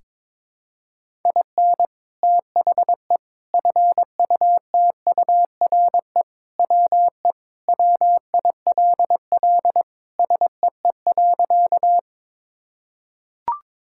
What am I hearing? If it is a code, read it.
Morse code, heard as a keyed tone, about 22 wpm, 700 Hz. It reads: IN THE FUTURE WE WILL SEE.
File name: practice / 22wpm / Sentences with 3-Word Phrases - 22wpm